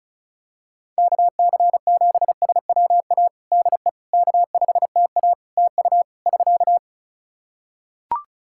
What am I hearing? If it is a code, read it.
Morse code, heard as a keyed tone, about 34 wpm, 700 Hz. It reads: KC7SWA DE K5TA TU <SK>
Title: Practice QSO - Exchange 4 - Ending - 35wpm